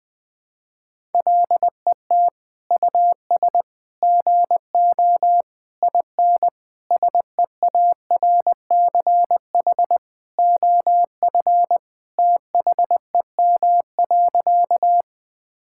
Morse code, heard as a keyed tone, about 20 wpm, 700 Hz. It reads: LET US GO IN SEARCH OF THEM.
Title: Sentences from Top 1000 Words - Review Part 5 - 20wpm